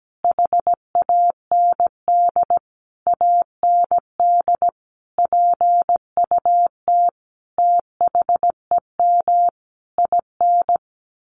Morse code, heard as a keyed tone, about 17 wpm, 700 Hz. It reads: HAND AND PUT THEM IN
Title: Sentences from Top 1300 Words 17wpm